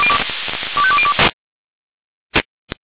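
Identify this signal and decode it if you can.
Five-tone selective call (ZVEI), heard as quick runs of five tones, tones of about 70 ms. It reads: 25202
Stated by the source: Received on an SDR, ZVEI five-tone selective call.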